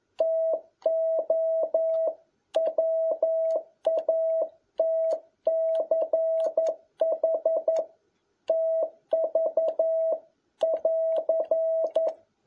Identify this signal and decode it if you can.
Morse code, heard as a keyed tone, about 11 wpm, 650 Hz. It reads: TO WATCH TV <AR>
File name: jh8cbh / 2021cw